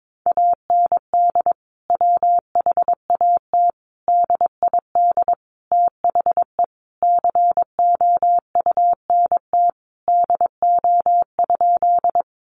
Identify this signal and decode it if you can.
Morse code, heard as a keyed tone, about 22 wpm, 700 Hz. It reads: AND WHAT DID THE COUNT DO?
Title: Sentences from Top 1500 Words 22wpm